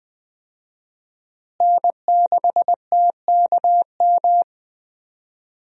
Morse code, heard as a keyed tone, about 20 wpm, 700 Hz. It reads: N6TKM
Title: USA Call Signs - 5 Characters - Repeated 2x - 20wpm